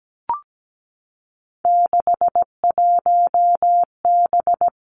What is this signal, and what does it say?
Morse code, heard as a keyed tone, about 17 wpm, 700 Hz. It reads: 61B
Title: Three Challenging Letter-Number Combinations 17wpm